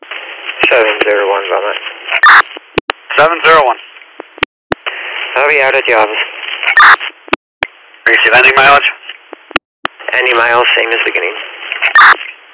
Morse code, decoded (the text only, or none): none